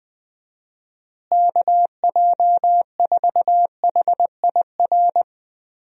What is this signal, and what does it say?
Morse code, heard as a keyed tone, about 21 wpm, 700 Hz. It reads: KJ4HIR
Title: USA Call Signs - 6 Characters - Repeated 3x - 20wpm